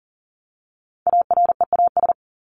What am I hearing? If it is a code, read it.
Morse code, heard as a keyed tone, about 40 wpm, 700 Hz. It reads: AREAS
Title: ES:S Suffix - Top 100 Words with Base Words 40wpm